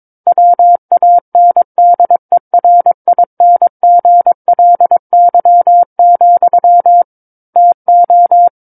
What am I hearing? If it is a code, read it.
Morse code, heard as a keyed tone, about 22 wpm, 700 Hz. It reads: WANDERINGLY, TO
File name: Book One / 13-Chapter 13 - 22 WPM